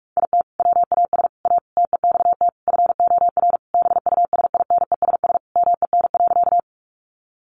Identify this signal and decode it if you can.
Morse code, heard as a keyed tone, about 44 wpm, 700 Hz. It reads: IT WAS A TEXT FOR BUSINESS MEN.
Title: Sentences from Top 1200 Words 45wpm